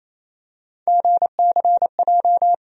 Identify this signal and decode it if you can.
Morse code, heard as a keyed tone, about 27 wpm, 700 Hz. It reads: GCJ
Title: USA Call Sign Postfixes - 3 Letters 28wpm